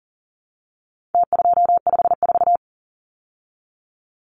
Morse code, heard as a keyed tone, about 40 wpm, 700 Hz. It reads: T254
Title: Numbers - Contest Style - 4-Digit 40wpm